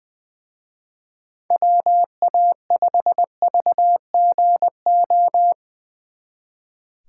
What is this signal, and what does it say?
Morse code, heard as a keyed tone, about 23 wpm, 700 Hz. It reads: WA5VGO
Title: Speed Racing - USA 2x3 Call Signs 20wpm